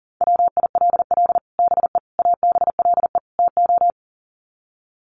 Morse code, heard as a keyed tone, about 40 wpm, 700 Hz. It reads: WILL BE ABLE TO
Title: Sets of 4 Words 40wpm